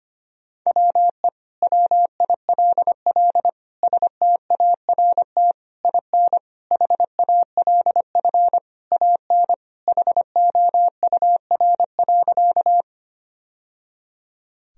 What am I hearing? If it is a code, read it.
Morse code, heard as a keyed tone, about 25 wpm, 700 Hz. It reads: WE WILL START IN HALF AN HOUR.
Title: Sentences from Top 1000 Words - Review Part 1 - 25wpm